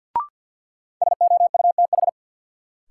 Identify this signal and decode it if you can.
Morse code, heard as a keyed tone, about 46 wpm, 700 Hz. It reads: SOUTH
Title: Sentences from Top 400 Words - New Word Drill - 50wpm